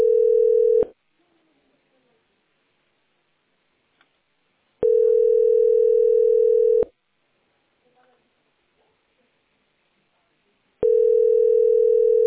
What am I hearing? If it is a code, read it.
Ringback tone (North America), 440 + 480 Hz.